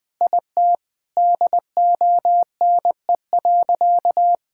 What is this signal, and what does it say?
Morse code, heard as a keyed tone, about 21 wpm, 700 Hz. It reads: IT DONE.
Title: Sentences from Top 1000 Words - Review Part 5 - 20wpm